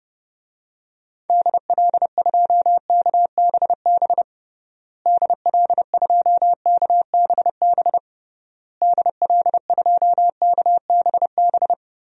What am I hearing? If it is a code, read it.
Morse code, heard as a keyed tone, about 30 wpm, 700 Hz. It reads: DL2KBB DL2KBB DL2KBB
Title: DXCC-100 Call Signs - 6 Characters - Repeated 3x - 30wpm